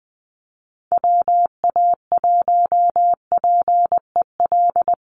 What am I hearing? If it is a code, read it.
Morse code, heard as a keyed tone, about 20 wpm, 700 Hz. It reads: WA1PEL
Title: USA Call Signs 20wpm